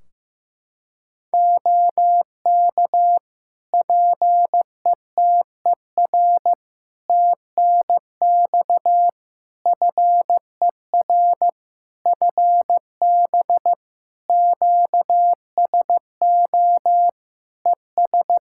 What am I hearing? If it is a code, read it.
Morse code, heard as a keyed tone, about 15 wpm, 700 Hz. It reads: OK PETER TNX FER FB QSO ES
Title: Practice QSO - Exchange 4 - Ending - 15wpm